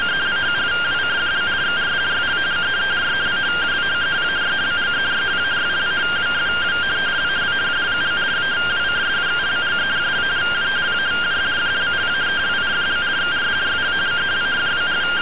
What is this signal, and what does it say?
Two-tone teleprinter signal, about 170 Hz shift.
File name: findings / MR AXIAL fr1 120.gen-11 rtty rx